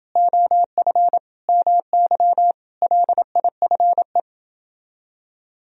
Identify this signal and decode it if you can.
Morse code, heard as a keyed tone, about 27 wpm, 700 Hz. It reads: OF MY LIFE